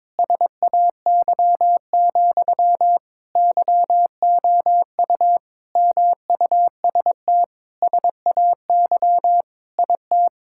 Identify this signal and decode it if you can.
Morse code, heard as a keyed tone, about 22 wpm, 700 Hz. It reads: SAY, YOU MUST SAY IT
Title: Sentences from Top 1600 Words 22wpm